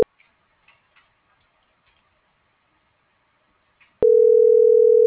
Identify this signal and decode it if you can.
Ringback tone (North America), 440 + 480 Hz.